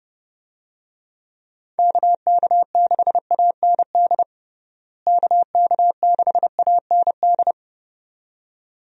Morse code, heard as a keyed tone, about 30 wpm, 700 Hz. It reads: KK6AND KK6AND
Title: USA Call Signs - 6 Characters - Repeated 2x - 30wpm